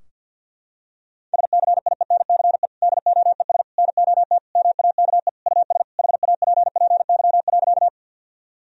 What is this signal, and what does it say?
Morse code, heard as a keyed tone, about 52 wpm, 700 Hz. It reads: SCIENCE DOES NOT MAKE US HAPPY.
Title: Sentences from Top 1000 Words - Review Part 2 - 50wpm